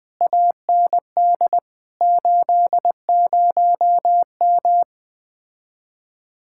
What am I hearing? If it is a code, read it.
Morse code, heard as a keyed tone, about 20 wpm, 700 Hz. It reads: AND 80M
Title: Speed-Racing Common QSO Phrases - 17wpm